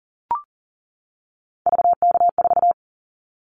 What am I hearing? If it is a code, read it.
Morse code, heard as a keyed tone, about 40 wpm, 700 Hz. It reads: VK4
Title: Callsign Prefixes for Top 50 DXCC - 40wpm